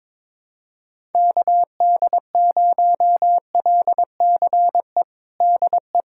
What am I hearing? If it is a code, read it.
Morse code, heard as a keyed tone, about 21 wpm, 700 Hz. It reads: KD0LCE DE
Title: USA Callsign DE Callsign - 6 Characters - 22wpm